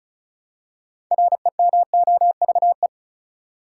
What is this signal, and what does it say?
Morse code, heard as a keyed tone, about 36 wpm, 700 Hz. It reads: REMOVE
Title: Six Letter Words - 35wpm